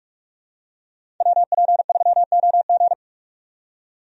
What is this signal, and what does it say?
Morse code, heard as a keyed tone, about 47 wpm, 700 Hz. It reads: WP3OG
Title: USA Call Signs - 5 Characters - 45wpm